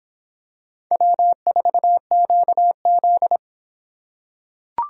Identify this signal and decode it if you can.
Morse code, heard as a keyed tone, about 26 wpm, 700 Hz. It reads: W4QZ